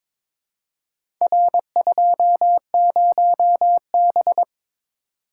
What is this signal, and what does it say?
Morse code, heard as a keyed tone, about 22 wpm, 700 Hz. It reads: R20B
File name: practice / 22wpm / Four Letter-Number Combinations - 22wpm